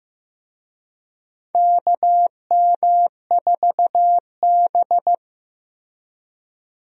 Morse code, heard as a keyed tone, about 15 wpm, 700 Hz. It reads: KM4B
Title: USA Call Signs - 4 Characters - Repeated 3x - 15wpm